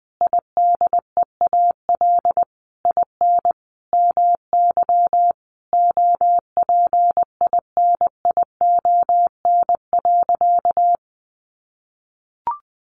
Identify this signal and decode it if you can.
Morse code, heard as a keyed tone, about 20 wpm, 700 Hz. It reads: IDEAL IN MY OPINION.
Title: Sentences from Top 1400 Words 20wpm